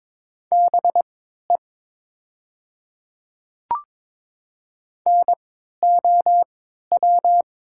Morse code, heard as a keyed tone, about 22 wpm, spaced as for 12 wpm, 700 Hz. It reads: BE NOW
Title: 3x Character Spacing - Top 100 Words - 22wpm